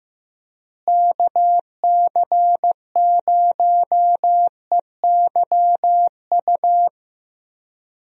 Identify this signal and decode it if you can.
Morse code, heard as a keyed tone, about 15 wpm, 700 Hz. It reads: KC0EYU
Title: USA Call Signs - 6 Characters - Repeated 3x - 15wpm